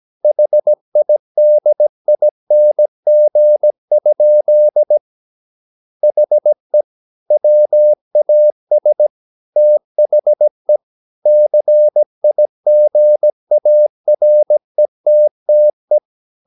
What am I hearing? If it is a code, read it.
Morse code, heard as a keyed tone, about 17 wpm, 600 Hz. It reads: HIDING? HE WAS THE CIGARETTE
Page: Romanen Never af Ken Follet på engelsk.